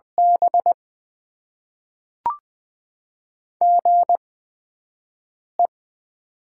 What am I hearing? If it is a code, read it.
Morse code, heard as a keyed tone, about 20 wpm, 700 Hz. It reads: B G E
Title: Top 20 QSO Elements - 8x Character Spacing - 20wpm